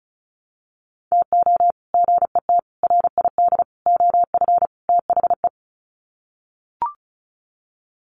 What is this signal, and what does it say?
Morse code, heard as a keyed tone, about 34 wpm, 700 Hz. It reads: TO GET RID OF THE
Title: Sets of 5 Words 35wpm